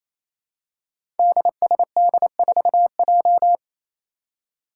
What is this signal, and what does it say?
Morse code, heard as a keyed tone, about 28 wpm, 700 Hz. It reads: DSD4J
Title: Five Challenging Letter-Number Combinations 28wpm